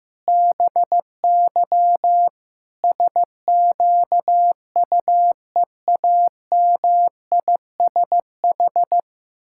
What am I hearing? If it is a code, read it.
Morse code, heard as a keyed tone, about 15 wpm, 700 Hz. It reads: BY SQUEAMISH
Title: Pangram Challenge v2 - 15wpm